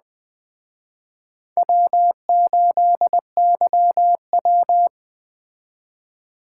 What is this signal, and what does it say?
Morse code, heard as a keyed tone, about 20 wpm, 700 Hz. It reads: W8YW
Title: Speed Racing - USA 1x2 Call Signs 17wpm